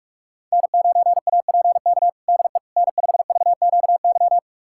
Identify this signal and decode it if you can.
Morse code, heard as a keyed tone, about 46 wpm, 700 Hz. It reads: N9APK DE N5VQY
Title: USA Callsign DE Callsign - 5 Characters - 45wpm